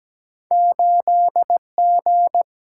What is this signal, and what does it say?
Morse code, heard as a keyed tone, about 17 wpm, 700 Hz. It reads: 8G
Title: Single-Letter Single-Number 17wpm